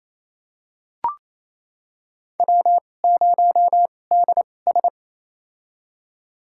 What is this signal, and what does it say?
Morse code, heard as a keyed tone, about 28 wpm, spaced as for 20 wpm, 700 Hz. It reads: W0DS
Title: USA Call Signs - 4 Characters - 2x Character Spacing - 28wpm